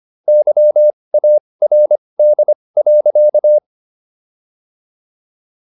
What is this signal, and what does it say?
Morse code, heard as a keyed tone, about 25 wpm, spaced as for 20 wpm, 600 Hz. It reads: YARD.